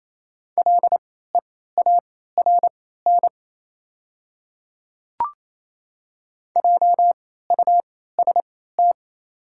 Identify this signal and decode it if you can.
Morse code, heard as a keyed tone, about 28 wpm, spaced as for 16 wpm, 700 Hz. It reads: LEARN JUST
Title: 3x Character Spacing - Top 300 Words - 28wpm